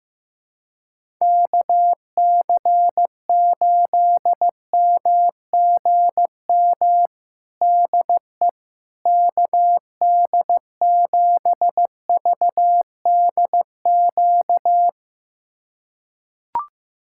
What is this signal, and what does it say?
Morse code, heard as a keyed tone, about 15 wpm, 700 Hz. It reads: KC8MGM DE KD7VDQ
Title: USA Callsign DE Callsign - 6 Characters - 15wpm